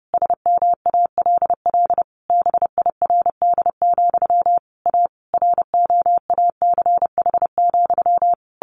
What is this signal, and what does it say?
Morse code, heard as a keyed tone, about 30 wpm, 700 Hz. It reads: SMALL BIRD, A ROACH,
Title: A Sound Of Thunder 30wpm